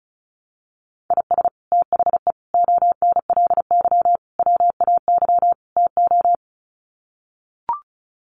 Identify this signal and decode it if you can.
Morse code, heard as a keyed tone, about 34 wpm, 700 Hz. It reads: IS THE ONLY WAY TO
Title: Sets of 5 Words Encore 35wpm